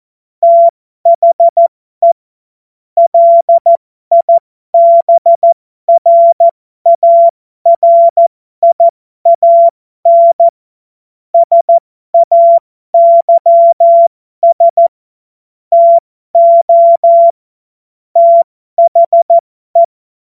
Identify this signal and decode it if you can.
Morse code, heard as a keyed tone, about 14 wpm, spaced as for 12 wpm, 680 Hz. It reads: THE LIBRARIAN SAYS TO THE
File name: Morsekurs / Witze